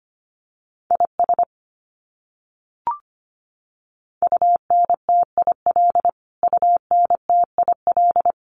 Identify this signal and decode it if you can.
Morse code, heard as a keyed tone, about 25 wpm, 700 Hz. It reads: IS UNTIL UNTIL
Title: Top 500 Words Repeated 25wpm